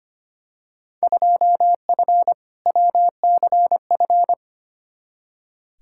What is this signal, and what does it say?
Morse code, heard as a keyed tone, about 25 wpm, 700 Hz. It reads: 2F WCF
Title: Field Day Exchanges Encore - 25wpm